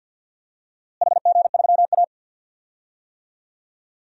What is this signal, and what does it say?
Morse code, heard as a keyed tone, about 50 wpm, 700 Hz. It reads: HG3A
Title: DXCC-100 Call Signs - 4 Characters - Repeated 3x - 50wpm